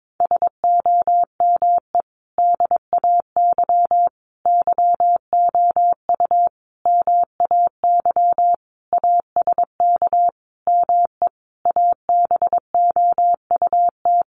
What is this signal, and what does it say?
Morse code, heard as a keyed tone, about 22 wpm, 700 Hz. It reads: SOME DAY YOU MAY ASK ME ABOUT
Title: Sentences from Top 200 Words 22wpm